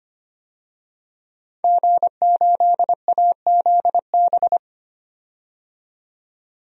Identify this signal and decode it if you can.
Morse code, heard as a keyed tone, about 25 wpm, 700 Hz. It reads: G8AZB
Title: DXCC-100 Call Signs - 5 Characters - 25wpm